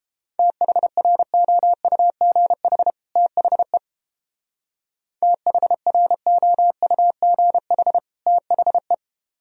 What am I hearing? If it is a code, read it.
Morse code, heard as a keyed tone, about 31 wpm, 700 Hz. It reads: THROUGH THE THROUGH THE